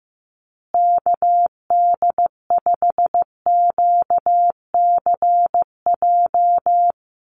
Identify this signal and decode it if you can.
Morse code, heard as a keyed tone, about 15 wpm, 700 Hz. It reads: KD5QCJ
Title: USA Call Signs Repeated 3x 15wpm